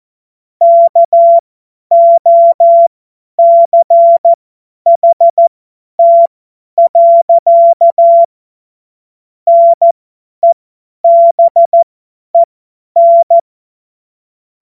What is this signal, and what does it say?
Morse code, heard as a keyed tone, about 14 wpm, spaced as for 10 wpm, 680 Hz. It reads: KOCHT. NEBEN
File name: Morsekurs / Witze / Witz 01. März 2023 50 F70